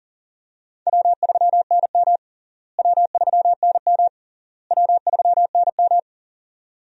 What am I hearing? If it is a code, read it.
Morse code, heard as a keyed tone, about 37 wpm, 700 Hz. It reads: W3NM W3NM W3NM
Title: USA Call Signs - 4 Characters - Repeated 3x - 40wpm